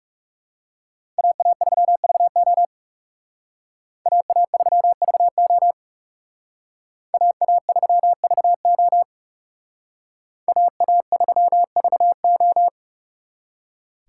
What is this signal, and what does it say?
Morse code, heard as a keyed tone, about 36 wpm, 700 Hz. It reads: AA3VO AA3VO AA3VO AA3VO
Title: Speed Racing - USA 2x2 Call Signs 30wpm